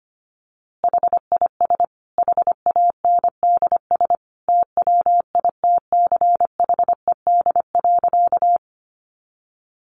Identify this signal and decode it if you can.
Morse code, heard as a keyed tone, about 25 wpm, 700 Hz. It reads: HIS HANDS TWITCHED.
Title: A Sound Of Thunder 25wpm